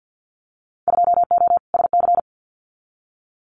Morse code, heard as a keyed tone, about 50 wpm, 700 Hz. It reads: 2O SC
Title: Winter Field Day Exchanges Encore 50wpm